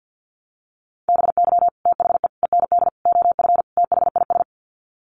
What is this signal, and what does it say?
Morse code, heard as a keyed tone, about 50 wpm, 700 Hz. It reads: BY THE END OF THIS
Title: Sets of 5 Words Encore 50wpm